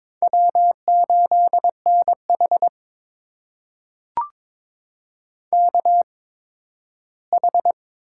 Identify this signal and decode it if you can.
Morse code, heard as a keyed tone, about 22 wpm, spaced as for 13 wpm, 700 Hz. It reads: W8NH KH
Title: USA Call Signs - 4 Characters - 8x Character Spacing - 22wpm